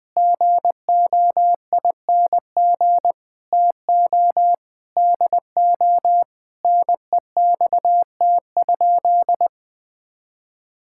Morse code, heard as a keyed tone, about 21 wpm, 700 Hz. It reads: GOING TO DO NEXT?
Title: Sentences with 5-Word Phrases - 20wpm